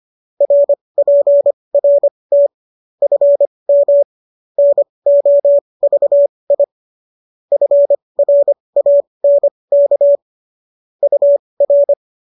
Morse code, heard as a keyed tone, about 25 wpm, spaced as for 20 wpm, 570 Hz. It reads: RPRT FM NOVI FRANK UR
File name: qso221 20